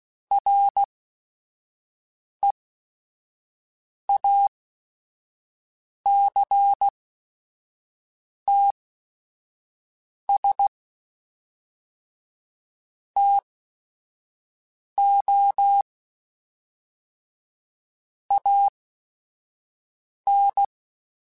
Morse code, heard as a keyed tone, about 16 wpm, spaced as for 5 wpm, 800 Hz. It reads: REACTS TO AN
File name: Technical Subjects / Technical Subjects-mp3s / XL @5